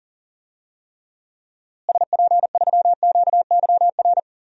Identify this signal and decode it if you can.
Morse code, heard as a keyed tone, about 40 wpm, 700 Hz. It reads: SP3QYR
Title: DXCC-100 Call Signs - 6 Characters - Repeated 3x - 40wpm